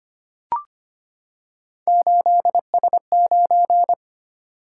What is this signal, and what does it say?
Morse code, heard as a keyed tone, about 25 wpm, 700 Hz. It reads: 8S9
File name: Three Letter-Number Combinations - 25wpm